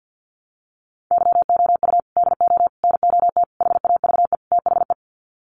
Morse code, heard as a keyed tone, about 50 wpm, 700 Hz. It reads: YOU DO NOT HAVE THE
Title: Sets of 5 Words 50wpm